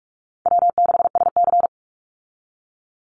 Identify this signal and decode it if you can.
Morse code, heard as a keyed tone, about 44 wpm, 700 Hz. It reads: W6SC
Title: Easy Call Signs Repeated 3x - 45wpm